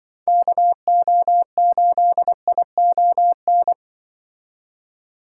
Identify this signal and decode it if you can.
Morse code, heard as a keyed tone, about 25 wpm, 700 Hz. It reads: KO8ION